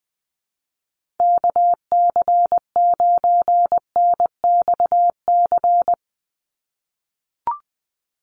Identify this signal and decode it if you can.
Morse code, heard as a keyed tone, about 20 wpm, 700 Hz. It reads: KC9NXC
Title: USA Call Signs 20wpm